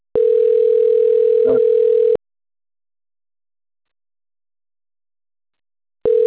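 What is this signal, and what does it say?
Ringback tone (North America), 440 + 480 Hz.
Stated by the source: Outgoing phone call recorded at Cornelio Procopio.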